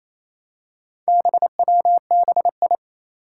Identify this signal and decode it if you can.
Morse code, heard as a keyed tone, about 28 wpm, 700 Hz. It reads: BWBI
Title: Four Challenging Letter-Number Combinations 28wpm